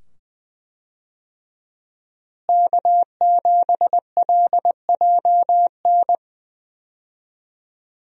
Morse code, heard as a keyed tone, about 20 wpm, 700 Hz. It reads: K7LJN
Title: USA Call Signs - 5 Characters - Repeated 2x - 20wpm